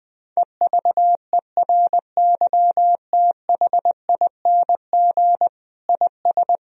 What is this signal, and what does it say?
Morse code, heard as a keyed tone, about 21 wpm, 700 Hz. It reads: EVERYTHING IS
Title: Sentences from Top 1000 Words - Review Part 2 - 20wpm